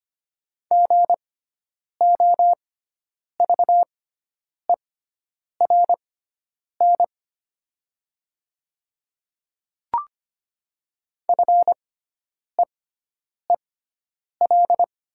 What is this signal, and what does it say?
Morse code, heard as a keyed tone, about 25 wpm, spaced as for 9 wpm, 700 Hz. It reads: GOVERN FEEL
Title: 6x Character Spacing - Top 500 Words - 25wpm